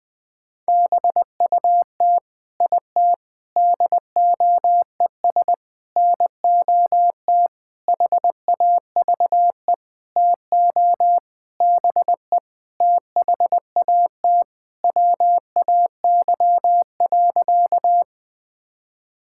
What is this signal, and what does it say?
Morse code, heard as a keyed tone, about 19 wpm, 700 Hz. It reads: BUT IT DOES NOT HAVE TO BE THAT WAY.
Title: Sentences with 8-Word Phrases - Spoken Beginning - 20wpm